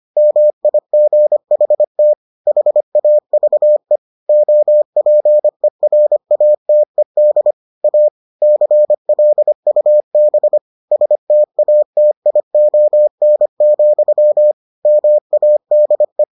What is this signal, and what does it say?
Morse code, heard as a keyed tone, about 25 wpm, 600 Hz. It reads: MIGHT HAVE OPERATED A CLUB STATION, MADE